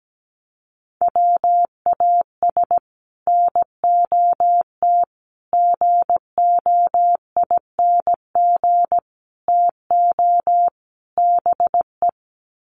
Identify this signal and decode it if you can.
Morse code, heard as a keyed tone, about 17 wpm, 700 Hz. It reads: WAS NOT GOING TO BE
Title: Sets of 5 Words Encore 17wpm